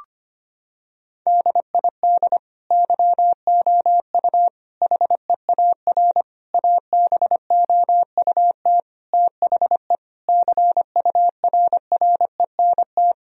Morse code, heard as a keyed tone, about 25 wpm, 700 Hz. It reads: DID YOU HEAR ABOUT THE CURRENT
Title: Sentences from Top 1000 Words - Review Part 4 - 25wpm